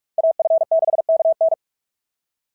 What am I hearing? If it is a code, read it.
Morse code, heard as a keyed tone, about 44 wpm, 650 Hz. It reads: AF6KN
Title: SOTA QSO Practice 45wpm